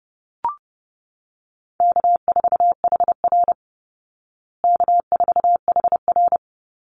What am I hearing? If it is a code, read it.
Morse code, heard as a keyed tone, about 30 wpm, 700 Hz. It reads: K4HR K4HR
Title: Easy Call Signs Repeated 3x - 30wpm